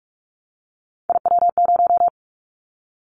Morse code, heard as a keyed tone, about 46 wpm, 700 Hz. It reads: IW0
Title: Callsign Prefixes for Top 50 DXCC - 45wpm